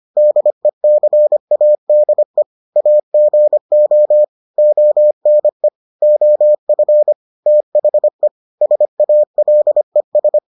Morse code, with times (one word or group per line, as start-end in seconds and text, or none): 0.2-2.4 DECADE
2.8-4.2 AGO
4.6-5.7 ONE
6.0-7.1 OF
7.5-8.3 THE
8.6-10.4 SALES